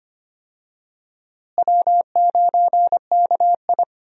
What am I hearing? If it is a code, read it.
Morse code, heard as a keyed tone, about 25 wpm, 700 Hz. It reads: W9KI
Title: USA Call Signs - 4 Characters - Repeated 3x - 25wpm